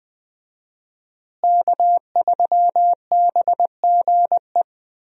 Morse code, heard as a keyed tone, about 20 wpm, 700 Hz. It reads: K3BGE
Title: USA Call Signs - 5 Characters - Repeated 2x - 20wpm